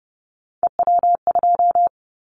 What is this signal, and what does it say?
Morse code, heard as a keyed tone, about 30 wpm, 700 Hz. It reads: EW2
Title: Callsign Prefixes for Top 100 DXCC - 30wpm